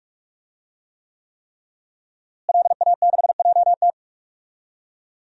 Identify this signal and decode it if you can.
Morse code, heard as a keyed tone, about 47 wpm, 700 Hz. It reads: LA6JT
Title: DXCC-100 Call Signs - 5 Characters - Repeated 3x - 45wpm